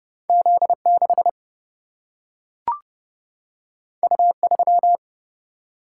Morse code, heard as a keyed tone, about 31 wpm, 700 Hz. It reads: Z 6 U 3
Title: Single-Letter Single-Number 30wpm